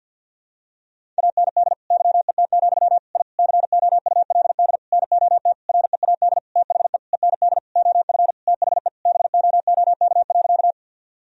Morse code, heard as a keyed tone, about 53 wpm, 700 Hz. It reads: AND YET, I COULD NOT READ THE END OF THE BOOK.
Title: Sentences from Top 1000 Words - Review Part 1 - 50wpm